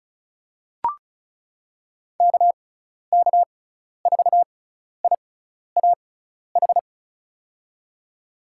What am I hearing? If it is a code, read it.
Morse code, heard as a keyed tone, about 36 wpm, spaced as for 12 wpm, 700 Hz. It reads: KK4IAH
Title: USA Call Signs - 6 Characters - 6x Character Spacing - 35wpm